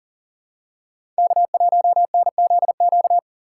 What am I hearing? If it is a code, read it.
Morse code, heard as a keyed tone, about 38 wpm, 700 Hz. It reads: K1NZQ
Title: USA Call Signs - 5 Characters - 4x Character Spacing - 40wpm